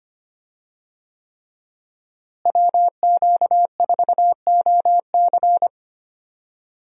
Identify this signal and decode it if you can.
Morse code, heard as a keyed tone, about 25 wpm, 700 Hz. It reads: WQ4OC
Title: USA Call Signs - 5 Characters - Repeated 3x - 25wpm